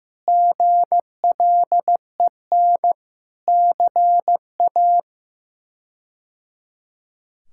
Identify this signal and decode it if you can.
Morse code, heard as a keyed tone, about 15 wpm, 700 Hz. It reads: GLEN CA
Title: K1USN SST QSO Practice - 15wpm